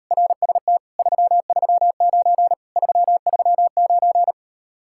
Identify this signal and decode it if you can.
Morse code, heard as a keyed tone, about 38 wpm, 700 Hz. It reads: RST 339 339